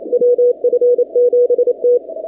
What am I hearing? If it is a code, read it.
Morse code, heard as a keyed tone, about 27 wpm, 490 Hz. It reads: WF7T
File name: 2019-cw-wpx-kt5j-k5pi